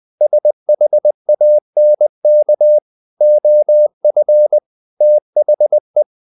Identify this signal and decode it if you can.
Morse code, heard as a keyed tone, about 20 wpm, 600 Hz. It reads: SHANK OF THE
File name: the old man and the sea / the old man and the sea part 1 20wpm